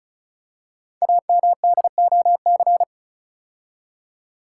Morse code, heard as a keyed tone, about 36 wpm, 700 Hz. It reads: AMDOC
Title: Code Groups - 5 Letters - 35wpm